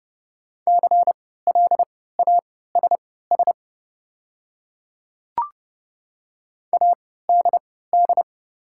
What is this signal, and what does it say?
Morse code, heard as a keyed tone, about 30 wpm, spaced as for 17 wpm, 690 Hz. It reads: CLASS ADD
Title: 3x Character Spacing - Top 500 Words - 30wpm